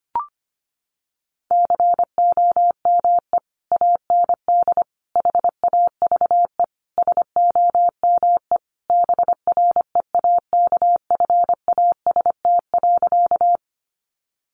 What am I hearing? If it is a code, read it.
Morse code, heard as a keyed tone, about 25 wpm, 700 Hz. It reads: COME AND HAVE SOME BREAKFAST.
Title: Sentences from Top 1500 Words 25wpm